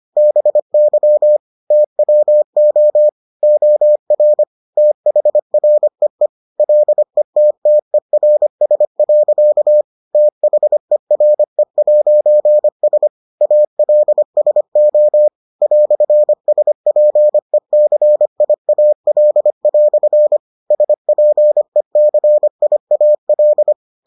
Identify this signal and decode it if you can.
Morse code, heard as a keyed tone, about 25 wpm, 600 Hz. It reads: BY TWO OR THREE LETTERS. THERE'S ALSO "SPECIAL" SPECIAL